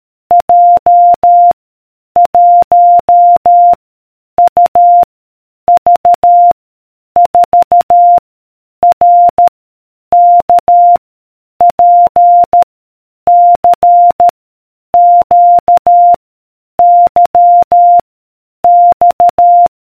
Morse code, heard as a keyed tone, about 13 wpm, 700 Hz. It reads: J 1 U V 4 R K P C Q Y X